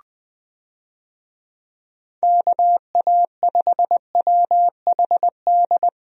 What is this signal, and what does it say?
Morse code, heard as a keyed tone, about 20 wpm, 700 Hz. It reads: KA5WHD
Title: USA Call Signs - 6 Characters - 20wpm